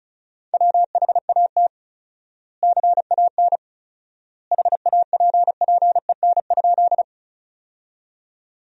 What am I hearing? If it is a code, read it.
Morse code, heard as a keyed tone, about 34 wpm, 700 Hz. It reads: WHAT CAN HAPPEN?
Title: 4x Word-Spacing - Sentences from Top 400 Words - 35wpm